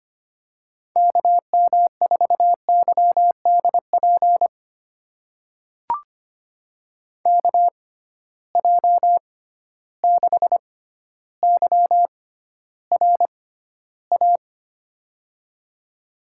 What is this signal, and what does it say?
Morse code, heard as a keyed tone, about 25 wpm, spaced as for 13 wpm, 700 Hz. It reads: KM4YDP KJ6YRA
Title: USA Call Signs - 6 Characters - 6x Character Spacing - 25wpm